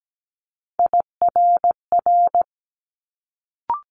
Morse code, heard as a keyed tone, about 17 wpm, 700 Hz. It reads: IRR
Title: Multi-letter Phonemes Quick Fire 17wpm